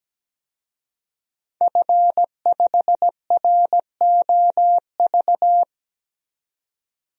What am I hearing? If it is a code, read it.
Morse code, heard as a keyed tone, about 17 wpm, 700 Hz. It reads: F5ROV
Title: DXCC-100 Call Signs - 5 Characters - 17wpm